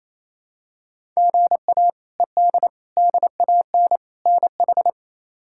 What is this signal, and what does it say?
Morse code, heard as a keyed tone, about 28 wpm, 700 Hz. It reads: GA ED DAN NH
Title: K1USN SST QSO Practice - 28wpm